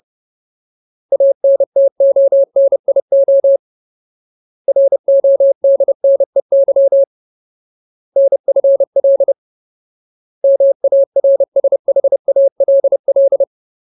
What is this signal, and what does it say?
Morse code, heard as a keyed tone, about 30 wpm, 550 Hz. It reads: ANTONIO RODNEY NFL MARSHALL
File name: Ultimate Word List / 30WPM Ultimate